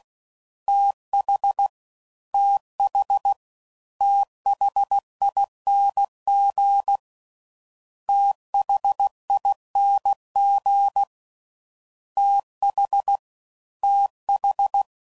Morse code, heard as a keyed tone, about 16 wpm, 800 Hz. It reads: TH TH THING THING TH TH